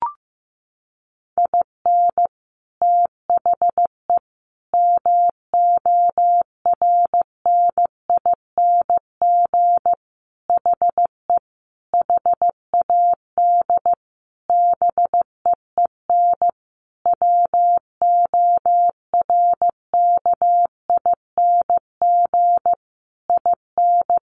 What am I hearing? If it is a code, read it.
Morse code, heard as a keyed tone, about 14 wpm, 700 Hz. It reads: IN THE MORNING HE HAD BEEN WORKING IN
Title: Sentences from Top 1200 Words 15wpm